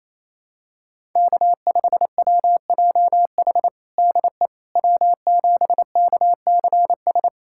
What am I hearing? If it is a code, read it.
Morse code, heard as a keyed tone, about 28 wpm, 700 Hz. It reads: K5WJH DE W7KCS
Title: USA Callsign DE Callsign - 5 Characters - 28wpm